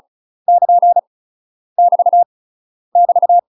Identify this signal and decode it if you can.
Morse code, heard as a keyed tone, about 36 wpm, 700 Hz. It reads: <KN> <BT> <BT>
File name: UK MASTER QSOs TRAINING / 35WPM / UK MASTER QSOs 0010